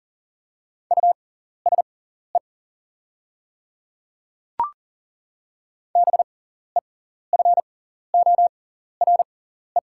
Morse code, heard as a keyed tone, about 40 wpm, spaced as for 14 wpm, 700 Hz. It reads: USE BEFORE
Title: 6x Character Spacing - Top 200 Words - 40wpm